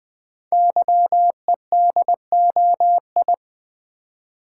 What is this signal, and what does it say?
Morse code, heard as a keyed tone, about 20 wpm, 700 Hz. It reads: YEDOI
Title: Code Groups - 5 Letters - 20wpm